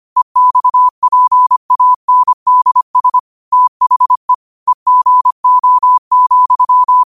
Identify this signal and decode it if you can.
Morse code, heard as a keyed tone, about 25 wpm, 1000 Hz. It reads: EXPANDS THE EGO,
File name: qotdfiles / 25wpm